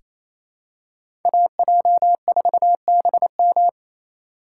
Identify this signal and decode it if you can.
Morse code, heard as a keyed tone, about 28 wpm, 700 Hz. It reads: AJ4BM
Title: USA Call Signs - 5 Characters - 4x Character Spacing - 28wpm